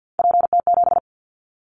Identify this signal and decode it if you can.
Morse code, heard as a keyed tone, about 50 wpm, 700 Hz. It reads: PT7
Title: Callsign Prefixes for Top 100 DXCC - 50wpm